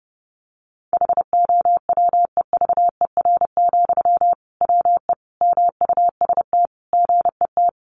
Morse code, heard as a keyed tone, about 30 wpm, 700 Hz. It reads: HOWEVER, WE MUST GET
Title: Sentences from Top 600 Words 30wpm